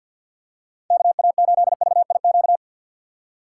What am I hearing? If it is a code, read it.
Morse code, heard as a keyed tone, about 53 wpm, 700 Hz. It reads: KA8VIQ
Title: USA Call Signs - 6 Characters - 6x Character Spacing - 50wpm